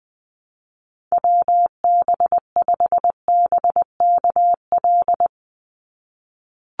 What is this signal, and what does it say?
Morse code, heard as a keyed tone, about 20 wpm, 700 Hz. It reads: WB5BKL
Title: NA SOTA Chaser Callsigns 20wpm